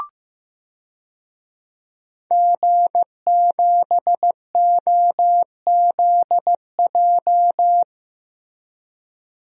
Morse code, heard as a keyed tone, about 15 wpm, 700 Hz. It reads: G7OZJ
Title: DXCC-100 Call Signs - 5 Characters - Repeated 3x - 15wpm